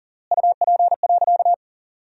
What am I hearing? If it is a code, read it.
Morse code, heard as a keyed tone, about 40 wpm, 700 Hz. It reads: UP.
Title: 3x Word-Spacing - Sentences from Top 500 Words - 40wpm